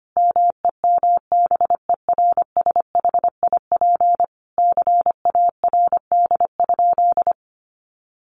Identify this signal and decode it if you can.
Morse code, heard as a keyed tone, about 25 wpm, 700 Hz. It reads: MEMBERSHIP CARD?
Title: Sentences from Top 1400 Words 25wpm